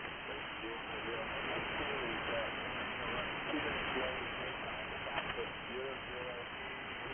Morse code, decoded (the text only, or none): none